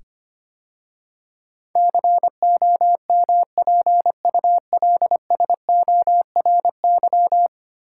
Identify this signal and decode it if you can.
Morse code, heard as a keyed tone, about 25 wpm, 700 Hz. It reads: COMPULSORY
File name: 10 Letter Words - First 3 Letters Given - 25wpm